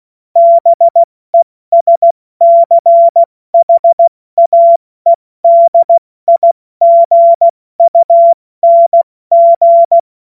Morse code, heard as a keyed tone, about 16 wpm, spaced as for 14 wpm, 680 Hz. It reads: BESCHAEDIGUNG
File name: Morsekurs / Witze / Witz 02. August 2022 70 F80